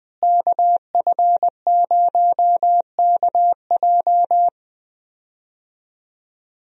Morse code, heard as a keyed tone, about 20 wpm, 700 Hz. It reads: KF0KJ
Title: USA Call Signs - 5 Characters - Repeated 2x - 20wpm